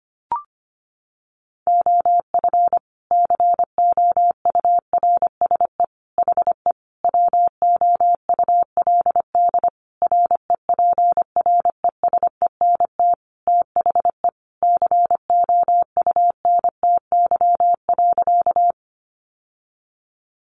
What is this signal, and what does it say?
Morse code, heard as a keyed tone, about 25 wpm, 700 Hz. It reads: OF COURSE HE WOULD REPRESENT THE COUNTY.
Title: Sentences from Top 1200 Words 25wpm